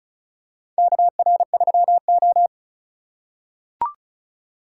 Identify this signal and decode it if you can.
Morse code, heard as a keyed tone, about 34 wpm, 700 Hz. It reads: KR3O
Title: USA Call Signs - 4 Characters - 2x Character Spacing - 35wpm